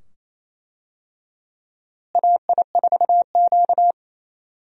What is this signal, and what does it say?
Morse code, heard as a keyed tone, about 28 wpm, 700 Hz. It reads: AI4Q
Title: USA Call Signs - 4 Characters - Repeated 2x - 28wpm